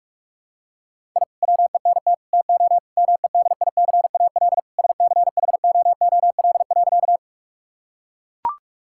Morse code, heard as a keyed tone, about 43 wpm, 700 Hz. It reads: I WENT TO MEDICAL SCHOOL.
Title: Sentences from Top 1000 Words - Review Part 5 - 45wpm